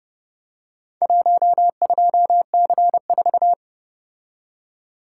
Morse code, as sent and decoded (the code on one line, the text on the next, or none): .---- ..--- -.-. ....-
12C4